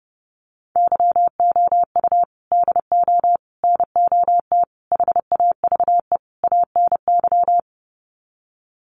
Morse code, heard as a keyed tone, about 30 wpm, 700 Hz. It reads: YOU DO NOT HAVE ANY
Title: Sets of 5 Words Encore 30wpm